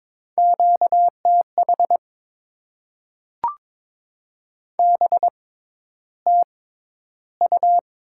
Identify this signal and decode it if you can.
Morse code, heard as a keyed tone, about 21 wpm, spaced as for 11 wpm, 700 Hz. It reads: QTH BTU
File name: Top 20 QSO Elements - 6x Character Spacing - 22wpm